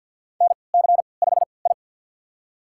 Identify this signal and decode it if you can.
Morse code, heard as a keyed tone, about 48 wpm, spaced as for 28 wpm, 700 Hz. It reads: NC5I
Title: USA Call Signs - 4 Characters - 3x Character Spacing - 50wpm